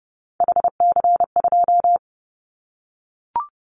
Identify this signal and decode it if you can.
Morse code, heard as a keyed tone, about 30 wpm, 700 Hz. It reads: HC2
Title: Callsign Prefixes for Top 100 DXCC - 30wpm